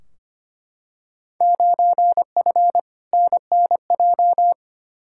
Morse code, heard as a keyed tone, about 25 wpm, 700 Hz. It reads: 9F NNJ
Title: Field Day Exchanges Encore - 25wpm